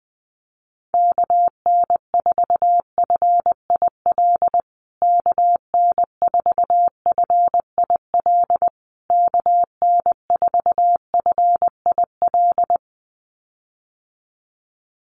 Morse code, heard as a keyed tone, about 20 wpm, 700 Hz. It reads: KN4FIL KN4FIL KN4FIL
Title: USA Call Signs Repeated 3x 20wpm